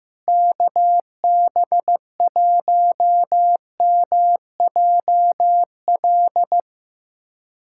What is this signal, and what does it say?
Morse code, heard as a keyed tone, about 15 wpm, 700 Hz. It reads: KB1MJL
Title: USA Callsign DE Callsign - 6 Characters - 15wpm